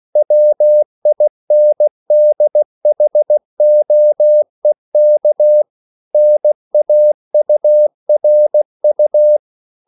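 Morse code, heard as a keyed tone, about 16 wpm, 600 Hz. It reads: WINDHOEK NAURU
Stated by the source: CW practice of national capitals at 16 words per minute, from ON6ZQ.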